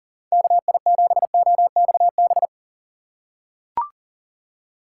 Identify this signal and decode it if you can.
Morse code, heard as a keyed tone, about 40 wpm, 700 Hz. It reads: KI7OXB
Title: USA Call Signs - 6 Characters - 6x Character Spacing - 40wpm